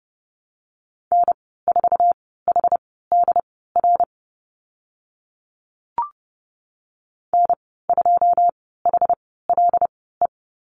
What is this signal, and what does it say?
Morse code, heard as a keyed tone, about 29 wpm, spaced as for 17 wpm, 700 Hz. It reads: N4HDR N2HLE
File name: Lesson 15g - Number 5 - Call Signs 30wpm